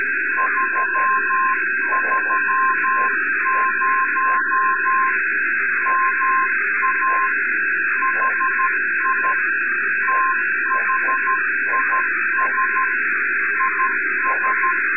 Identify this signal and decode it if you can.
Morse code, heard as a keyed tone, about 13 wpm, 1050 Hz, with noise. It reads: 330 MA HI THIS IS
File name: sat / RADIOSCAF-B 24Dec2011